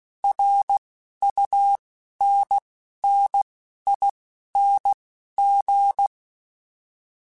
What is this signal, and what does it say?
Morse code, heard as a keyed tone, about 16 wpm, spaced as for 11 wpm, 800 Hz. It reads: RUNNING
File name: QSOs / QSOs-mp3s / WD0FBT 1@10